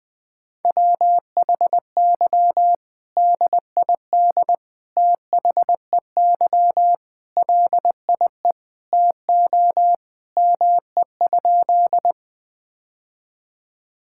Morse code, heard as a keyed tone, about 20 wpm, 700 Hz. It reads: WHY DID THEY LIE TO ME?
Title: Sentences from Top 1600 Words 20wpm